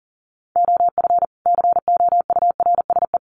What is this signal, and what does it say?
Morse code, heard as a keyed tone, about 40 wpm, 700 Hz. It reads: OF COURSE
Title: Sets of 2 Words 40wpm